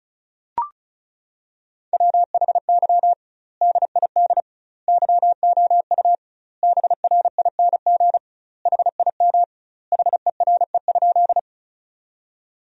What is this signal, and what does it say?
Morse code, heard as a keyed tone, about 36 wpm, 700 Hz. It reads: WHY DID YOU BRING HIM HERE?
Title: 2x Word-Spacing - Sentences from Top 500 Words - 35wpm